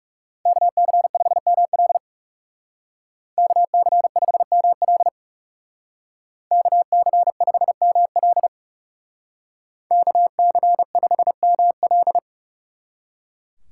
Morse code, heard as a keyed tone, about 37 wpm, 700 Hz. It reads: KC5ML KC5ML KC5ML KC5ML
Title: Speed Racing - USA Call Signs 30wpm